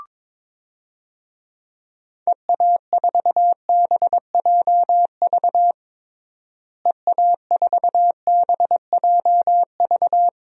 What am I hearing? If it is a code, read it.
Morse code, heard as a keyed tone, about 21 wpm, 700 Hz. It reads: EA4BJV EA4BJV
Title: DXCC-100 Call Signs - 6 Characters - Repeated 3x - 22wpm